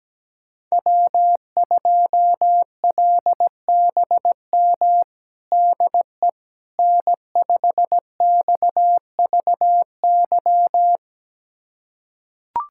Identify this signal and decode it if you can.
Morse code, heard as a keyed tone, about 17 wpm, 700 Hz. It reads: W2LBM DE N5XVY
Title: USA Callsign DE Callsign 17wpm